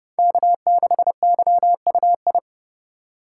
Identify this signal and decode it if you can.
Morse code, heard as a keyed tone, about 34 wpm, 700 Hz. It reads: K6YUI
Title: Speed Racing - USA 1x3 Call Signs 30wpm